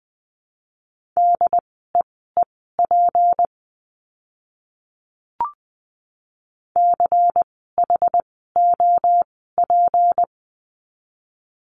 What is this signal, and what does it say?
Morse code, heard as a keyed tone, about 20 wpm, spaced as for 14 wpm, 700 Hz. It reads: DEEP CHOP
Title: 153 Lesson 25f - Letter p - Words 20wpm